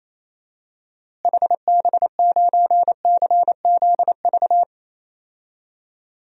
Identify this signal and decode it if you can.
Morse code, heard as a keyed tone, about 28 wpm, 700 Hz. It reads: HB9CZV
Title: DXCC-100 Call Signs - 6 Characters - Repeated 3x - 28wpm